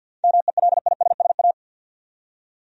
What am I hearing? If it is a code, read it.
Morse code, heard as a keyed tone, about 52 wpm, 700 Hz. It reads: MELISSA
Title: USA Common Names - 50wpm